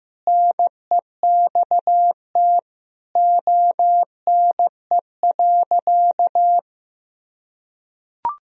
Morse code, heard as a keyed tone, about 15 wpm, 700 Hz. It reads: NEXT ONE.
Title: Sentences from Top 400 Words and Abbreviations 15wpm